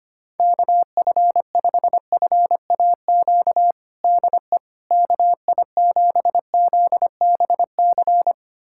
Morse code, heard as a keyed tone, about 25 wpm, 700 Hz. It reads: KF5FAQ DE KI7ZBC
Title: USA Callsign DE Callsign 25wpm